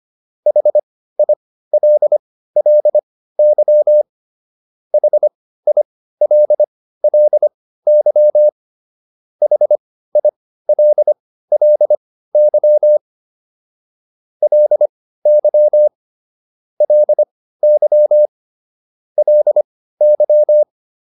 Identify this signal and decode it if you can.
Morse code, heard as a keyed tone, about 25 wpm, spaced as for 15 wpm, 600 Hz. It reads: HILLY HILLY HILLY LY LY LY